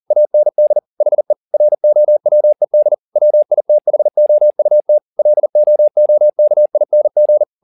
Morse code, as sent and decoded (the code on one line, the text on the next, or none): .- -. -.. / .... / .-. --- .-- . -.. / .-- .. - .... --- ..- - / .-.. --- --- -.- .. -. --.
AND HE ROWED WITHOUT LOOKING